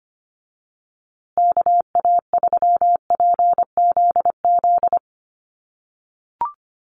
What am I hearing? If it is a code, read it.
Morse code, heard as a keyed tone, about 25 wpm, 700 Hz. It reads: KA3PZZ
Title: USA Call Signs Repeated 2x 25wpm